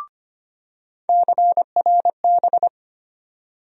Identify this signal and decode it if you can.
Morse code, heard as a keyed tone, about 25 wpm, 700 Hz. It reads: CRB
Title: USA Call Sign Postfixes - 3 Letters 25wpm